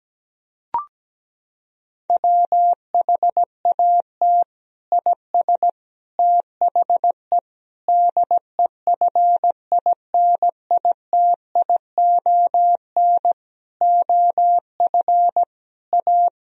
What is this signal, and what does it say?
Morse code, heard as a keyed tone, about 17 wpm, 700 Hz. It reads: WHAT IS THE DEFINITION OF A
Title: Sentences from Top 1000 Words - Review Part 4 - 17wpm